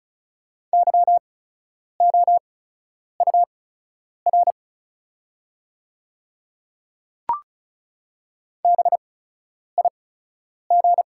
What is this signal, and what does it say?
Morse code, heard as a keyed tone, about 36 wpm, spaced as for 10 wpm, 700 Hz. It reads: YOUR BIG
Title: 8x Character Spacing - Top 300 Words - 35wpm